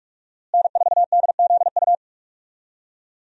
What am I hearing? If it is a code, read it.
Morse code, heard as a keyed tone, about 45 wpm, 700 Hz. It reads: N4DZU
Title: USA Call Signs - 5 Characters - Repeated 2x - 45wpm